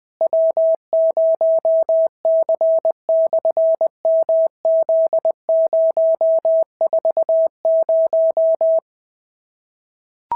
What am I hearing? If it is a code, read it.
Morse code, heard as a keyed tone, about 20 wpm, 650 Hz. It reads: W0C/MZ040
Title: SOTA QSO Practice 20wpm